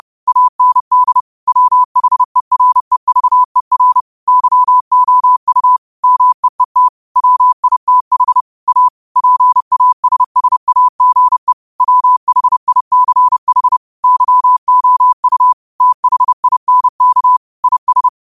Morse code, heard as a keyed tone, about 30 wpm, 1000 Hz. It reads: AND WHEREVER YOU MEET WITH A PASSAGE WHICH YOU THINK IS